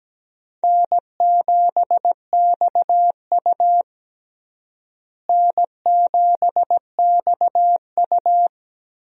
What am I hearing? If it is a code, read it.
Morse code, heard as a keyed tone, about 17 wpm, 700 Hz. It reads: N7XU N7XU
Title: USA Call Signs - 4 Characters - Repeated 3x - 17wpm